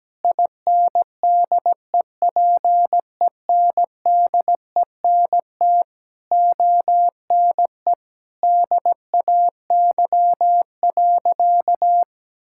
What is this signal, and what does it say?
Morse code, heard as a keyed tone, about 17 wpm, 700 Hz. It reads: INDEPENDENT ONE DAY.
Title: Sentences from Top 1000 Words - Review Part 6 - 17wpm